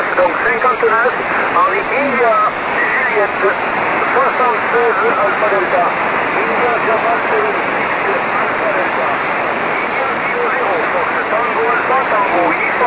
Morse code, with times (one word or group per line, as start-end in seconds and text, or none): none